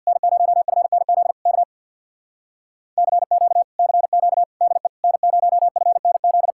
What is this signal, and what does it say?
Morse code, heard as a keyed tone, about 62 wpm, 700 Hz. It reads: N0FNZ K CQ CQ DE N0FNZ
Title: Speed-Racing Common QSO Phrases - 50wpm